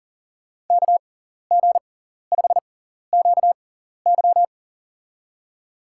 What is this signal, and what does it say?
Morse code, heard as a keyed tone, about 40 wpm, spaced as for 14 wpm, 700 Hz. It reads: KG5QY
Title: USA Call Signs - 5 Characters - 6x Character Spacing - 40wpm